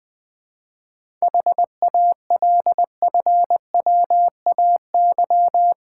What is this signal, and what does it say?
Morse code, heard as a keyed tone, about 20 wpm, 700 Hz. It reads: HALFWAY
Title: 7 Letter Words - First 3 Letters Given - 20wpm